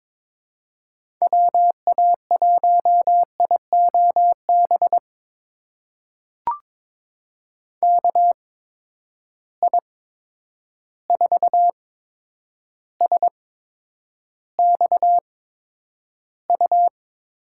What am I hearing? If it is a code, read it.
Morse code, heard as a keyed tone, about 22 wpm, spaced as for 9 wpm, 700 Hz. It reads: WA1IOB KI4SXU
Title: USA Call Signs - 6 Characters - 8x Character Spacing - 22wpm